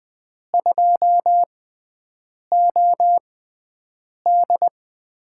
Morse code, heard as a keyed tone, about 20 wpm, spaced as for 7 wpm, 700 Hz. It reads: 2OD
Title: USA Call Signs - 4 Characters - 6x Character Spacing - 20wpm